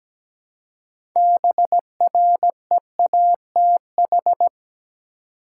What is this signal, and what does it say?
Morse code, heard as a keyed tone, about 17 wpm, 700 Hz. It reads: BREATH
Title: Six Letter Words - 17wpm